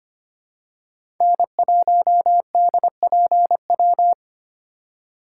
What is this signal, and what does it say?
Morse code, heard as a keyed tone, about 25 wpm, 700 Hz. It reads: N1DPW
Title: USA Call Signs - 5 Characters - 6x Character Spacing - 25wpm